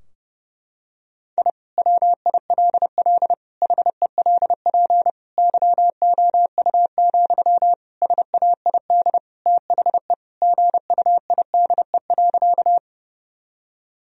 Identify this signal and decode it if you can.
Morse code, heard as a keyed tone, about 30 wpm, 700 Hz. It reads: I WILL HELP YOU, SAID THE GUIDE.
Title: Sentences from Top 1000 Words - Review Part 3 - 30wpm